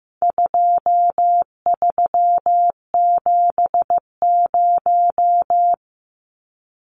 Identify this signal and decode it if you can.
Morse code, heard as a keyed tone, about 15 wpm, 700 Hz. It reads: 2370
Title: Numbers - 4-Digit 15wpm